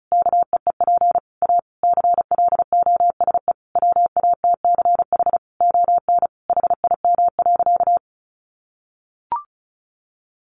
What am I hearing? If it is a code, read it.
Morse code, heard as a keyed tone, about 34 wpm, 700 Hz. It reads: KEEP A CLOSE WATCH ON HIM.
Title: Sentences from Top 1300 Words 35wpm